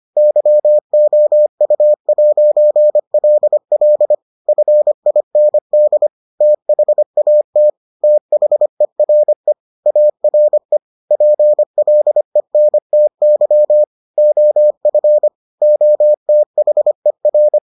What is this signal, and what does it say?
Morse code, heard as a keyed tone, about 25 wpm, 600 Hz. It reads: YOU'LL FIND THAT THERE ARE PLENTY OF OTHER